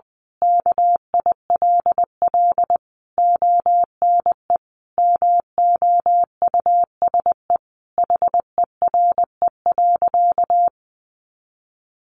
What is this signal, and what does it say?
Morse code, heard as a keyed tone, about 19 wpm, 700 Hz. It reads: KILL ONE MOUSE HERE.
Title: A Sound Of Thunder 20wpm